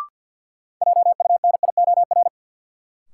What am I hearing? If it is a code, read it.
Morse code, heard as a keyed tone, about 52 wpm, 700 Hz. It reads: JUNIOR
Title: Sentences from Top 1500 Words - New Word Drill Alt - 50wpm